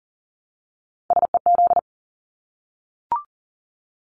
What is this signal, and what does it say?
Morse code, heard as a keyed tone, about 40 wpm, 700 Hz. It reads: SEZ
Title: Abbreviations Repeated 3x - 40wpm